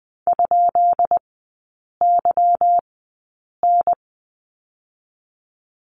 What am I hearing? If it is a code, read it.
Morse code, heard as a keyed tone, about 20 wpm, 690 Hz. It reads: ? Y N
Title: 145 Lesson 24e - Letter y - Three Characters 20wpm